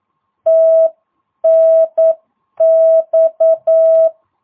Morse code, heard as a keyed tone, about 9 wpm, 650 Hz. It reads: TNX
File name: jh8cbh / 2022cw